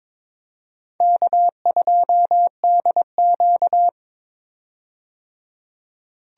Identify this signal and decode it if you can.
Morse code, heard as a keyed tone, about 22 wpm, 700 Hz. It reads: K2DQ
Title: USA Call Signs - 4 Characters - 22wpm